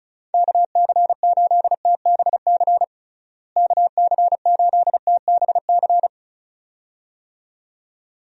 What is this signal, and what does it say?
Morse code, heard as a keyed tone, about 36 wpm, 700 Hz. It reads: KC8TBC KC8TBC
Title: USA Call Signs - 6 Characters - Repeated 2x - 35wpm